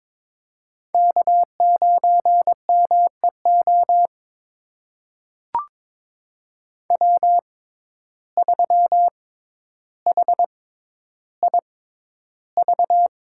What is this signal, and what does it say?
Morse code, heard as a keyed tone, about 22 wpm, spaced as for 11 wpm, 700 Hz. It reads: K9MEO W3HIV
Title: USA Call Signs - 5 Characters - 6x Character Spacing - 22wpm